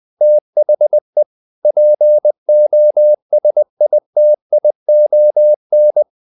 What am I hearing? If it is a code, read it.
Morse code, heard as a keyed tone, about 20 wpm, 600 Hz. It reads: THE POSITION
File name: the old man and the sea / the old man and the sea part 1 20wpm